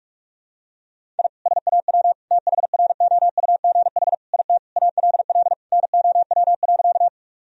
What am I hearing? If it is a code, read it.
Morse code, heard as a keyed tone, about 41 wpm, 700 Hz. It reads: I SAW THROUGH IT ALL NOW.
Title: Sentences from Top 1000 Words - Review Part 1 - 45wpm